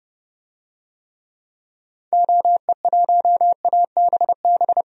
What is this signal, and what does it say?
Morse code, heard as a keyed tone, about 30 wpm, 700 Hz. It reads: OE1ABB
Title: DXCC-100 Call Signs - 6 Characters - 30wpm